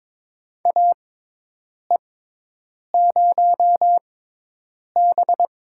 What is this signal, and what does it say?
Morse code, heard as a keyed tone, about 22 wpm, spaced as for 8 wpm, 700 Hz. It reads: AE0B
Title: USA Call Signs - 6x Character Spacing - 22wpm